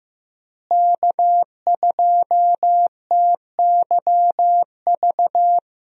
Morse code, heard as a keyed tone, about 15 wpm, 700 Hz. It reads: K2TYV
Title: USA Call Signs - 5 Characters - Repeated 2x - 15wpm